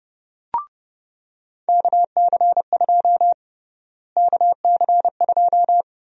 Morse code, heard as a keyed tone, about 30 wpm, 700 Hz. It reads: KC2 KC2
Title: USA Call Sign Prefixes - Repeated 2x - 30wpm